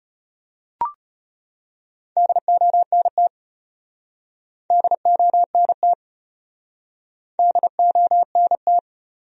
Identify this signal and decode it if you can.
Morse code, heard as a keyed tone, about 36 wpm, 700 Hz. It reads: DONT DONT DONT